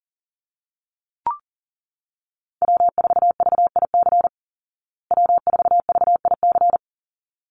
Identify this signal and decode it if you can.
Morse code, heard as a keyed tone, about 40 wpm, 700 Hz. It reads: W4VIC W4VIC
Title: Call Signs Repeated 3x - 40wpm